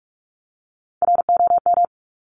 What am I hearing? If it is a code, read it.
Morse code, heard as a keyed tone, about 46 wpm, 700 Hz. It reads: ROM
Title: Top Three Letter Combinations in Words 45wpm